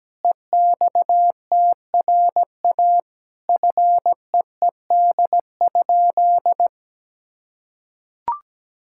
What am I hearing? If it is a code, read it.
Morse code, heard as a keyed tone, about 17 wpm, 700 Hz. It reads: EXTRA FEED?
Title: Sentences from Top 1600 Words 17wpm